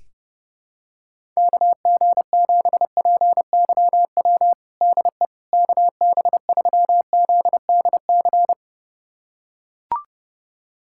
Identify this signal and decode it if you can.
Morse code, heard as a keyed tone, about 30 wpm, 700 Hz. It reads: KG7PYW DE KB3ZDC
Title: USA Callsign DE Callsign - 6 Characters - 30wpm